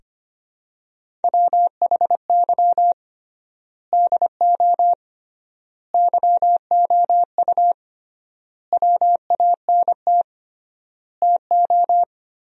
Morse code, heard as a keyed tone, about 25 wpm, 700 Hz. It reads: WHY DO YOU WANT TO
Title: 3x Word-Spacing - Sentences from Top 200 Words - 25wpm